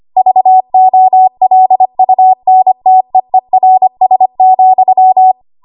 Morse code, heard as a keyed tone, about 25 wpm, 750 Hz. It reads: VOLUNTEERS,
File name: Archive / 25 WPM / 25WPM